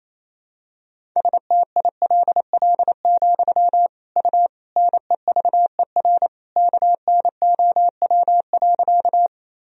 Morse code, heard as a keyed tone, about 28 wpm, 700 Hz. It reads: STILL, U NEVER KNOW.
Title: Sentences from Top 300 Words and Abbreviations 28wpm